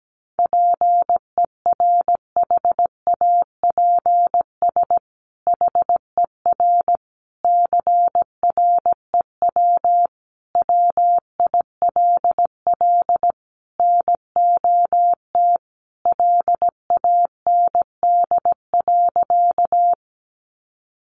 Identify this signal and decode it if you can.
Morse code, heard as a keyed tone, about 17 wpm, 700 Hz. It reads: PERHAPS HER CREW WILL NOT LAND.
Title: Sentences from Top 1500 Words 17wpm